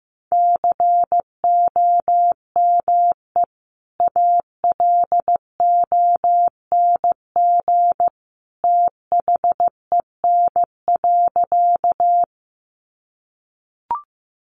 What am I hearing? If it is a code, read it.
Morse code, heard as a keyed tone, about 15 wpm, 700 Hz. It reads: COME ALONG THEN.
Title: Sentences from Top 700 Words 15wpm